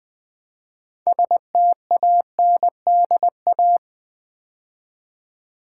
Morse code, heard as a keyed tone, about 21 wpm, 700 Hz. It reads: STANDA
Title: Build A Word - 2 Letters At A Time - 8-Letter Words - 20wpm